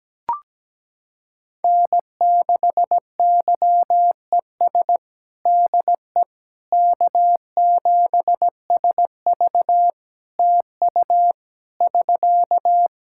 Morse code, heard as a keyed tone, about 17 wpm, 700 Hz. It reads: N6YES DE K7SV TU <SK>
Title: K1USN SST QSO Practice 17wpm